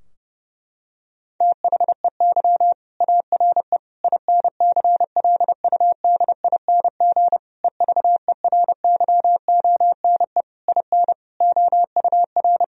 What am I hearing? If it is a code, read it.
Morse code, heard as a keyed tone, about 30 wpm, 700 Hz. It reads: THEY ARE INCLUDING EVERYONE IN OUR
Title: Sentences from Top 1000 Words - Review Part 4 - 30wpm